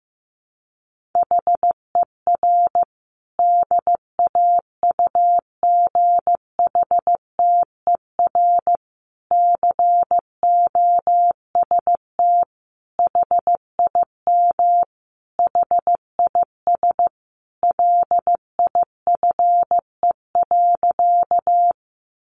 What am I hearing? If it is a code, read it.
Morse code, heard as a keyed tone, about 15 wpm, 700 Hz. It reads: HER DAUGHTER COST HIM HIS LIFE.
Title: Sentences from Top 1500 Words 15wpm